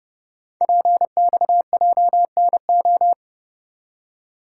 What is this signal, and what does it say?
Morse code, heard as a keyed tone, about 30 wpm, 690 Hz. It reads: PXJNO
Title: Code Groups - 5 Letters - 30wpm